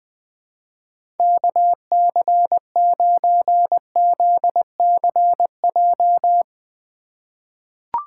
Morse code, heard as a keyed tone, about 20 wpm, 700 Hz. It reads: KC9ZCJ
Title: USA Call Signs - 6 Characters - 4x Character Spacing - 20wpm